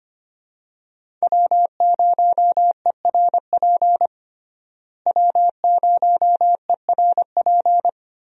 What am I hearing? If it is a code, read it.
Morse code, heard as a keyed tone, about 25 wpm, 700 Hz. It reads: W0ERP W0ERP
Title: USA Call Signs - 5 Characters - Repeated 2x - 25wpm